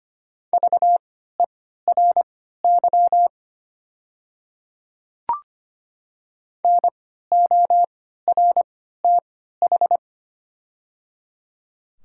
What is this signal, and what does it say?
Morse code, heard as a keyed tone, about 25 wpm, spaced as for 14 wpm, 700 Hz. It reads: VERY NORTH
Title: 3x Character Spacing - Top 300 Words - 25wpm